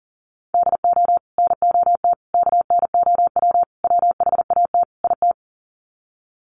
Morse code, heard as a keyed tone, about 40 wpm, 700 Hz. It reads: DO NOT KNOW WHAT IT
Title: Sets of 5 Words 40wpm